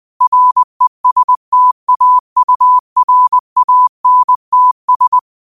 Morse code, heard as a keyed tone, about 20 wpm, 1000 Hz. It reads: RESTAURANTS
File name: qotdfiles / 20wpm